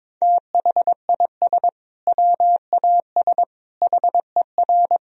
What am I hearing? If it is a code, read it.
Morse code, heard as a keyed tone, about 22 wpm, 700 Hz. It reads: THIS WAS HER
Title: Sentences from Top 1000 Words - Review Part 3 - 22wpm